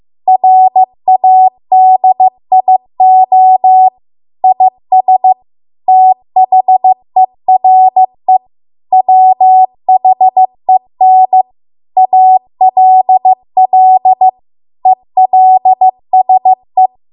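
Morse code, heard as a keyed tone, about 15 wpm, 750 Hz. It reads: RADIO IS THERE WHEN ALL ELSE
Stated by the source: Morse code practice at 15 wpm from the national amateur radio association's code archive.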